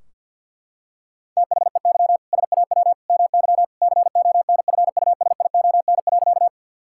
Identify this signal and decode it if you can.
Morse code, heard as a keyed tone, about 52 wpm, 700 Hz. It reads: THEY SAW MY CONFUSION.
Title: Sentences from Top 1600 Words 50wpm